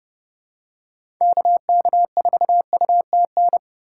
Morse code, heard as a keyed tone, about 30 wpm, 700 Hz. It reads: KK4UTN
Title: USA Call Signs - 6 Characters - 8x Character Spacing - 30wpm